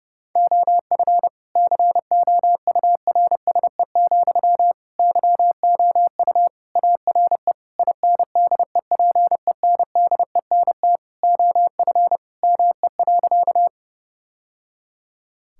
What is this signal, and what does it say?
Morse code, heard as a keyed tone, about 30 wpm, 700 Hz. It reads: OF COURSE, YOU ARE INDEPENDENT OF ME.
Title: Sentences from Top 1000 Words - Review Part 6 - 30wpm